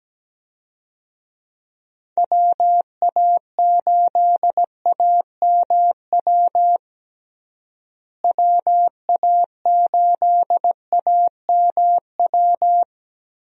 Morse code, heard as a keyed tone, about 17 wpm, 700 Hz. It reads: WA8AMW WA8AMW
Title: USA Call Signs - 6 Characters - Repeated 3x - 17wpm